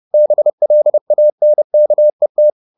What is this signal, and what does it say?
Morse code, heard as a keyed tone, about 30 wpm, 600 Hz. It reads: BLANKET
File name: the old man and the sea / the old man and the sea 1 30wpm